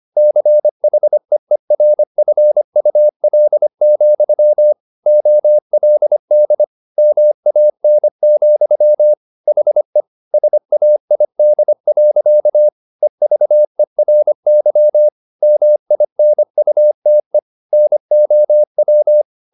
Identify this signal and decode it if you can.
Morse code, heard as a keyed tone, about 25 wpm, 600 Hz. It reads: CHEERFUL, OLD MAN, HE SAID. EVERY MINUTE NOW